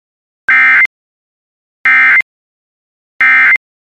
Emergency Alert System (SAME) tones, heard as harsh buzzing bursts. It is heard from 0.5 to 3.6 s.